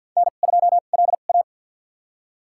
Morse code, heard as a keyed tone, about 49 wpm, spaced as for 36 wpm, 700 Hz. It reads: N2LA
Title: USA Call Signs - 4 Characters - 2x Character Spacing - 50wpm